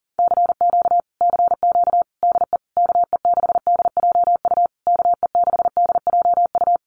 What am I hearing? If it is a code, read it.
Morse code, heard as a keyed tone, about 40 wpm, 700 Hz. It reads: CQ CQ DE KE6DJU KE6DJU
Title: Common QSO Phrases Encore 40wpm